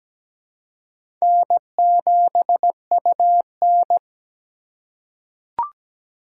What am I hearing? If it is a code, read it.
Morse code, heard as a keyed tone, about 17 wpm, 700 Hz. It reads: N7UN
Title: USA Call Signs - 4 Characters - 4x Character Spacing - 17wpm